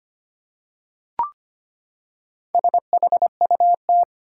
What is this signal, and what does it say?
Morse code, heard as a keyed tone, about 25 wpm, 700 Hz. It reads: SHUT
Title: Four Letter Words - 25wpm